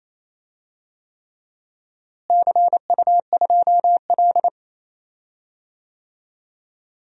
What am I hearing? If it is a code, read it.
Morse code, heard as a keyed tone, about 28 wpm, 700 Hz. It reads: CU2L
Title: DXCC-100 Call Signs - 4 Characters - Repeated 3x - 28wpm